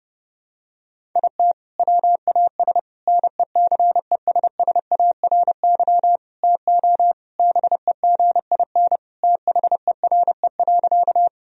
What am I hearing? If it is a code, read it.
Morse code, heard as a keyed tone, about 32 wpm, 700 Hz. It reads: IT WAS NECESSARY TO BEGIN THERE.
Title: Sentences from Top 1000 Words - Review Part 1 - 30wpm